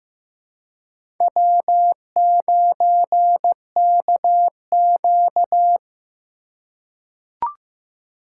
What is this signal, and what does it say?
Morse code, heard as a keyed tone, about 15 wpm, 700 Hz. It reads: W9KQ
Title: USA Call Signs - 4 Characters - 4x Character Spacing - 15wpm